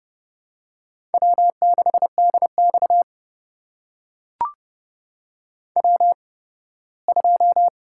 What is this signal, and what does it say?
Morse code, heard as a keyed tone, about 32 wpm, spaced as for 18 wpm, 700 Hz. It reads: W6DX W2
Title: USA Call Signs - 4 Characters - 8x Character Spacing - 30wpm